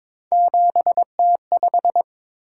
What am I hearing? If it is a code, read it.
Morse code, heard as a keyed tone, about 23 wpm, 700 Hz. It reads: 7T5
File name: practice / 22wpm / Three Letter-Number Combinations - 22wpm